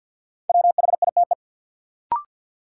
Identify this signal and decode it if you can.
Morse code, heard as a keyed tone, about 47 wpm, 700 Hz. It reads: WHITE
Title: Sentences from Top 300 Words - New Word Drill - 50wpm